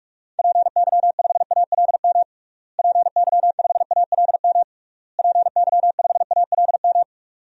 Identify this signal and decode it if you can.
Morse code, heard as a keyed tone, about 46 wpm, 700 Hz. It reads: PY5ALM PY5ALM PY5ALM
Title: DXCC-100 Call Signs - 6 Characters - Repeated 3x - 45wpm